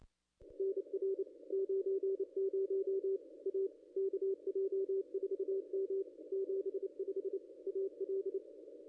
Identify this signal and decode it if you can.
Morse code, heard as a keyed tone, about 28 wpm, 380 Hz, with radio noise. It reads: NR 90 A KJ4M 75 AL
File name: contests / ss-cw-k5tr-op-n5rz -0200